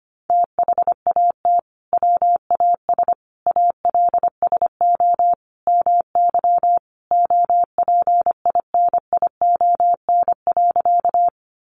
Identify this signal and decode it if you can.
Morse code, heard as a keyed tone, about 25 wpm, 700 Hz. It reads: THAT WAS ALSO MY OPINION.
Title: Sentences from Top 1200 Words 25wpm